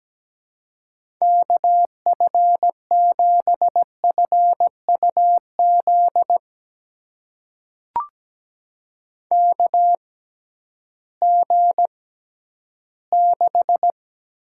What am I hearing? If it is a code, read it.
Morse code, heard as a keyed tone, about 17 wpm, spaced as for 11 wpm, 700 Hz. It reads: KF7FUZ KG6
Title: USA Call Signs - 6 Characters - 6x Character Spacing - 17wpm